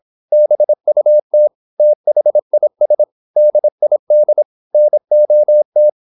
Morse code, heard as a keyed tone, about 26 wpm, 600 Hz. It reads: BUT THIS DID NOT
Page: Never 26wpm | CW med Gnister